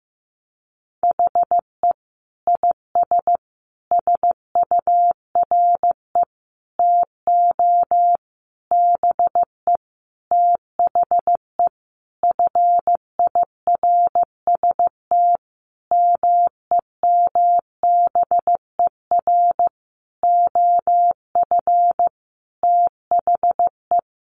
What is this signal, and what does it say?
Morse code, heard as a keyed tone, about 15 wpm, 700 Hz. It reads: HE IS SURE TO BE THE FIRST MEMBER OF THE
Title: Sentences from Top 1400 Words 15wpm